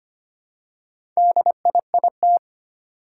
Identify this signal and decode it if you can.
Morse code, heard as a keyed tone, about 25 wpm, 700 Hz. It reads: DIIT
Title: Four Challenging Letter-Number Combinations 25wpm